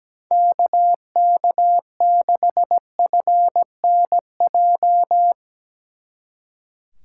Morse code, heard as a keyed tone, about 18 wpm, 700 Hz. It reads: KK6FNJ
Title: Speed Racing - USA 2x3 Call Signs 17wpm